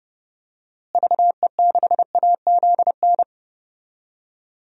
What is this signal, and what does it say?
Morse code, heard as a keyed tone, about 32 wpm, 700 Hz. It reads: VE6AZN
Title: DXCC-100 Call Signs - 6 Characters - Repeated 3x - 30wpm